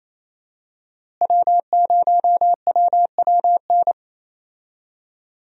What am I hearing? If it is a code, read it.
Morse code, heard as a keyed tone, about 28 wpm, 700 Hz. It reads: W0WWN
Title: USA Call Signs - 5 Characters - Repeated 3x - 28wpm